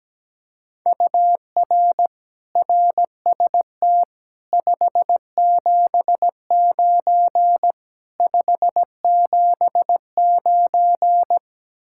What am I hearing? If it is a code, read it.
Morse code, heard as a keyed tone, about 17 wpm, 700 Hz. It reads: UR RST 579 579
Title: Speed-Racing Common QSO Phrases - 17wpm